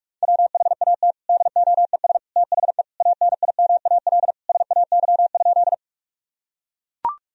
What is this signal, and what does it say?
Morse code, heard as a keyed tone, about 41 wpm, 700 Hz. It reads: WHAT DOES THE ANIMAL SAY?
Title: Sentences from Top 1000 Words - Review Part 1 - 45wpm